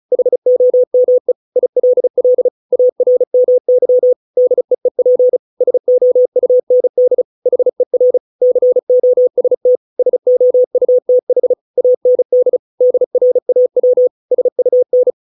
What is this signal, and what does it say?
Morse code, heard as a keyed tone, about 34 wpm, 500 Hz. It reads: HOME ILL ARMY DEEP SOUND HER COST SOUTH AND DRAW SUN